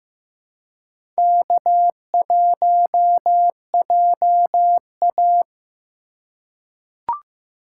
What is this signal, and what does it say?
Morse code, heard as a keyed tone, about 15 wpm, 700 Hz. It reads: K1JA
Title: USA Call Signs - 3x Character Spacing - 15wpm